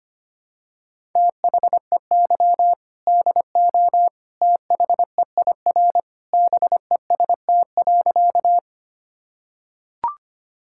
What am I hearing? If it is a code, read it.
Morse code, heard as a keyed tone, about 25 wpm, 700 Hz. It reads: THEY DO THEIR BEST.
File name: practice / 25wpm / Sentences with 2-Word Phrases - 25wpm